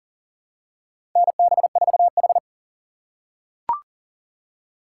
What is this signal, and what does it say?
Morse code, heard as a keyed tone, about 40 wpm, 700 Hz. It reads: NB4H
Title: USA Call Signs - 4 Characters - 2x Character Spacing - 40wpm